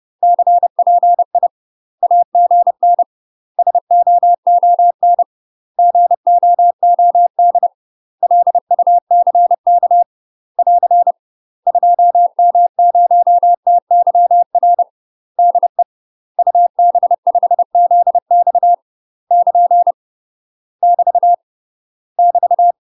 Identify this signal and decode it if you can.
Morse code, heard as a keyed tone, about 30 wpm, 700 Hz. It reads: CPI AGN SOON GOOD LUCK <AR> 2M0TYR DE UB5ZX <KN> <BT> <BT>
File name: UK MASTER QSOs TRAINING / 30WPM / UK MASTER QSOs 0045